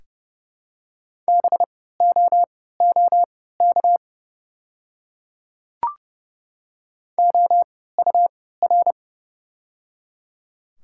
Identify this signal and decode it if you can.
Morse code, heard as a keyed tone, about 30 wpm, spaced as for 17 wpm, 690 Hz. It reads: BOOK OUR
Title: 3x Character Spacing - Top 300 Words - 30wpm